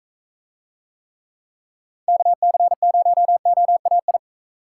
Morse code, heard as a keyed tone, about 44 wpm, 700 Hz. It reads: KC0OAI